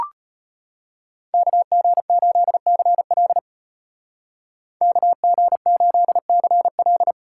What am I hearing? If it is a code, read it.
Morse code, heard as a keyed tone, about 34 wpm, 700 Hz. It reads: KG8CL KG8CL